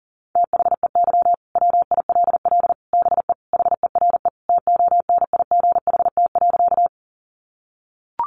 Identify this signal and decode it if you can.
Morse code, heard as a keyed tone, about 40 wpm, 700 Hz. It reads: THEY WILL BE HERE TONIGHT.
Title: Sentences from Top 600 Words 40wpm